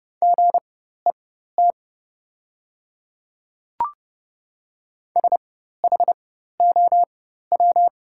Morse code, heard as a keyed tone, about 29 wpm, spaced as for 14 wpm, 700 Hz. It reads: GET SHOW
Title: 4x Character Spacing - Top 200 Words - 30wpm